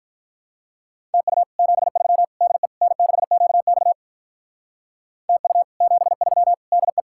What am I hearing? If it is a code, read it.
Morse code, heard as a keyed tone, about 52 wpm, 700 Hz. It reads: TU 73 DE N6QX TU 73 DE